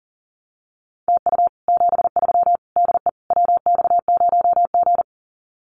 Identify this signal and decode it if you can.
Morse code, heard as a keyed tone, about 43 wpm, 700 Hz. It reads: TU 73 DE WX0G
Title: Common QSO Phrases Encore 40wpm